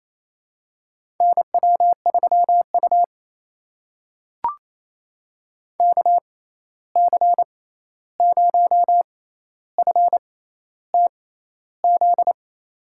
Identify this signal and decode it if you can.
Morse code, heard as a keyed tone, about 28 wpm, spaced as for 13 wpm, 700 Hz. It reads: NW3U KC0FTZ
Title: USA Call Signs - 6x Character Spacing - 28wpm